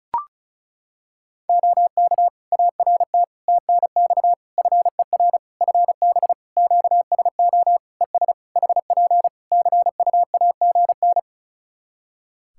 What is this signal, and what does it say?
Morse code, heard as a keyed tone, about 34 wpm, 700 Hz. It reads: OK ART TNX FER FB QSO ES HP CUAGN
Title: Practice QSO - Exchange 4 - Ending - 35wpm